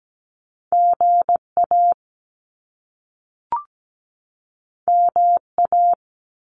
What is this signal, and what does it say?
Morse code, heard as a keyed tone, about 17 wpm, 700 Hz. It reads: GA MA
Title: US States 17wpm